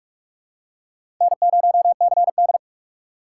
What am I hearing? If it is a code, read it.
Morse code, heard as a keyed tone, about 44 wpm, 690 Hz. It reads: N0CD
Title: USA Call Signs - 4 Characters - 3x Character Spacing - 45wpm